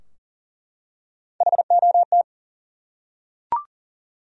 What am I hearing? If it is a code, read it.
Morse code, heard as a keyed tone, about 40 wpm, 700 Hz. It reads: HOT
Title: Top 100 Words Repeated 40wpm